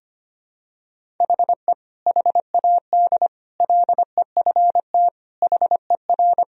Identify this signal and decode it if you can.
Morse code, heard as a keyed tone, about 25 wpm, 700 Hz. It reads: HE HAD LEFT HER
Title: Sentences from Top 1000 Words - Review Part 3 - 25wpm